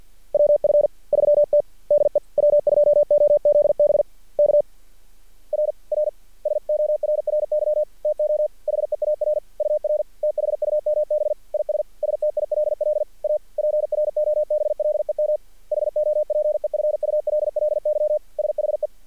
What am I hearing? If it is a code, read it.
Morse code, heard as a keyed tone, about 46 wpm, 600 Hz. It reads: WF3T DE W2OZB K R R SORRY TO HEAR UR THUMB IS STILL A PROBLEM HOPEFULLY SHE
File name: archived / CRSnet